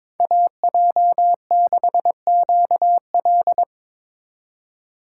Morse code, heard as a keyed tone, about 22 wpm, 700 Hz. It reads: AJ6QL
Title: K1USN SST QSO Practice - 22wpm